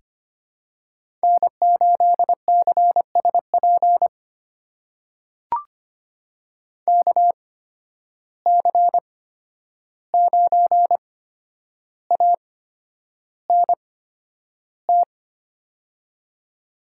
Morse code, heard as a keyed tone, about 25 wpm, spaced as for 10 wpm, 700 Hz. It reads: N8CSP KC9ANT
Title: USA Call Signs - 8x Character Spacing - 25wpm